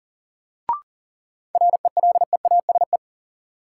Morse code, heard as a keyed tone, about 40 wpm, 700 Hz. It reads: RELEASE
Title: Sentences from Top 1100 Words - New Word Drill Alt - 40wpm